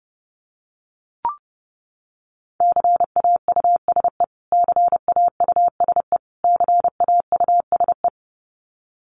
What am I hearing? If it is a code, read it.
Morse code, heard as a keyed tone, about 30 wpm, 700 Hz. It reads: CAUSE CAUSE CAUSE
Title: Top 500 Words Repeated 30wpm